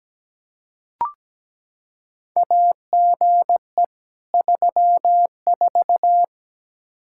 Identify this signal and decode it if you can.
Morse code, heard as a keyed tone, about 17 wpm, 700 Hz. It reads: AGE 34
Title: Practice QSO - Exchange 3 - More Info - 17wpm